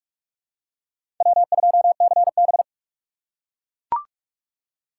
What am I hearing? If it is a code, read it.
Morse code, heard as a keyed tone, about 43 wpm, 700 Hz. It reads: W2CB
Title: USA Call Signs - 4 Characters - 4x Character Spacing - 45wpm